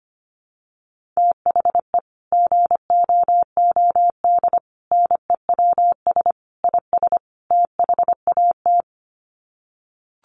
Morse code, heard as a keyed tone, about 25 wpm, 700 Hz. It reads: THE GOOD NEWS IS THAT
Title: Sets of 5 Words 25wpm